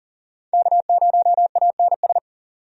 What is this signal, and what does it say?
Morse code, heard as a keyed tone, about 40 wpm, 700 Hz. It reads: K0ANS
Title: USA Call Signs - 5 Characters - 40wpm